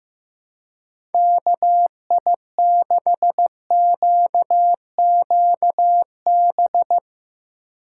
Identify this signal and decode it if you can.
Morse code, heard as a keyed tone, about 15 wpm, 700 Hz. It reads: KI6QQB
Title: USA Call Signs - 6 Characters - 4x Character Spacing - 15wpm